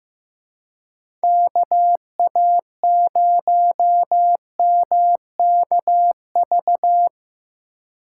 Morse code, heard as a keyed tone, about 16 wpm, 700 Hz. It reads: KA0MKV
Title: USA Call Signs - 6 Characters - 4x Character Spacing - 15wpm